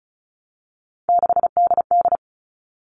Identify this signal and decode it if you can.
Morse code, heard as a keyed tone, about 34 wpm, 700 Hz. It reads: 6DD
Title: All Cut Numbers - 3-Digit 35wpm